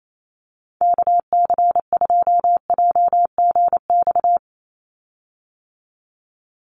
Morse code, heard as a keyed tone, about 28 wpm, 700 Hz. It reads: KC2JGX
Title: USA Call Signs 28wpm